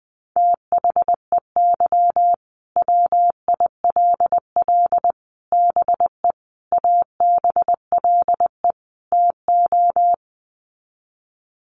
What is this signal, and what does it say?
Morse code, heard as a keyed tone, about 20 wpm, 700 Hz. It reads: THEY WILL BE ABLE TO
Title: Sets of 5 Words Encore 20wpm